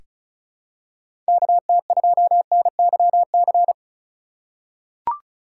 Morse code, heard as a keyed tone, about 36 wpm, 700 Hz. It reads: KT2NYC
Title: USA Call Signs - 6 Characters - 4x Character Spacing - 35wpm